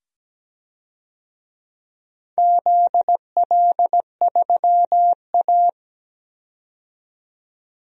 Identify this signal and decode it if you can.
Morse code, heard as a keyed tone, about 17 wpm, 700 Hz. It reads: ZL3A
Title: DXCC-100 Call Signs - 4 Characters - 17wpm